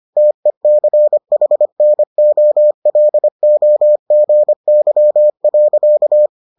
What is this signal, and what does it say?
Morse code, heard as a keyed tone, about 25 wpm, 600 Hz. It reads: TECHNOLOGY.